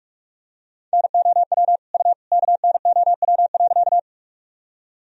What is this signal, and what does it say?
Morse code, heard as a keyed tone, about 46 wpm, 700 Hz. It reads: NOW U KNOW.
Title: Sentences from Top 100 Words and Abbreviations 45wpm